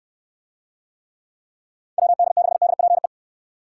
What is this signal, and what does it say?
Morse code, heard as a keyed tone, about 67 wpm, 700 Hz. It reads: WD6DPE